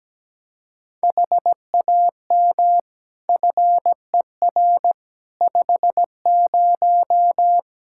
Morse code, heard as a keyed tone, about 17 wpm, 700 Hz. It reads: HAM FER 50
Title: Practice QSO - Exchange 3 - More Info - 17wpm